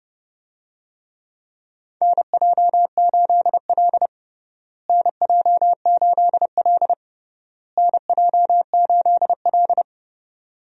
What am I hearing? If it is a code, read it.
Morse code, heard as a keyed tone, about 32 wpm, 700 Hz. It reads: NJ8L NJ8L NJ8L
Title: USA Call Signs - 4 Characters - Repeated 3x - 30wpm